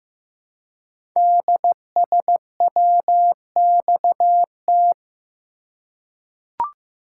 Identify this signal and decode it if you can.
Morse code, heard as a keyed tone, about 15 wpm, 700 Hz. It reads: DSWXT
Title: Code Groups - 5 Letters - 15wpm